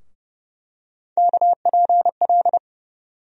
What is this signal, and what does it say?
Morse code, heard as a keyed tone, about 30 wpm, 700 Hz. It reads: KPL
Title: USA Call Sign Postfixes - 3 Letters 30wpm